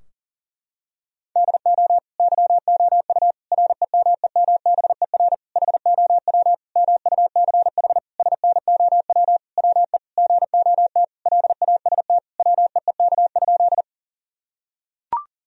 Morse code, heard as a keyed tone, about 37 wpm, 700 Hz. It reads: DO YOU REMEMBER HOW MUCH SNOW WE GOT LAST WEEK?
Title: Sentences from Top 1000 Words - Review Part 3 - 40wpm